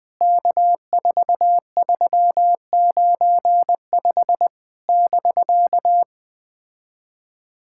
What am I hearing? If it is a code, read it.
Morse code, heard as a keyed tone, about 20 wpm, 700 Hz. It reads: K4395 <BK>
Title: POTA QSO Practice - 20wpm